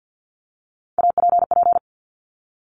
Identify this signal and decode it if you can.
Morse code, heard as a keyed tone, about 50 wpm, 700 Hz. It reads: APP
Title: Top Three Letter Combinations in Words 50wpm